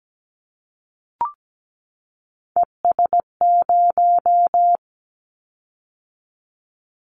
Morse code, heard as a keyed tone, about 17 wpm, 700 Hz. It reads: ES0
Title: Callsign Prefixes for Top 100 DXCC - 17wpm